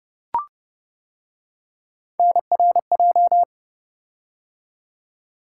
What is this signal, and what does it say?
Morse code, heard as a keyed tone, about 28 wpm, 700 Hz. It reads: NRJ